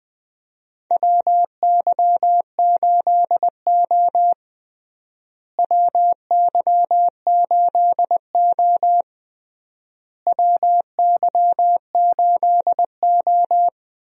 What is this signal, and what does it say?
Morse code, heard as a keyed tone, about 20 wpm, 700 Hz. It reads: WY8O WY8O WY8O
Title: USA Call Signs - 4 Characters - Repeated 3x - 20wpm